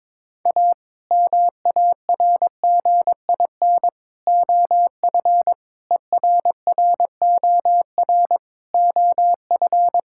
Morse code, heard as a keyed tone, about 22 wpm, 700 Hz. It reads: A MARGIN OF ERROR OF
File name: practice / 22wpm / Sets of 8 Words - 22wpm